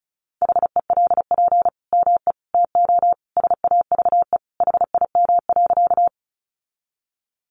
Morse code, heard as a keyed tone, about 34 wpm, 700 Hz. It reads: HELP ME TO SAVE HIM.
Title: Sentences from Top 1100 Words 35wpm